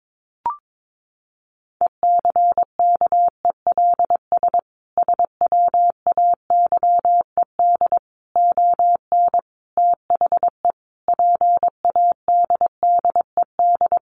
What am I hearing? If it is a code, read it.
Morse code, heard as a keyed tone, about 22 wpm, 700 Hz. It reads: ECKELS SWAYED ON THE PADDED
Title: A Sound Of Thunder 22wpm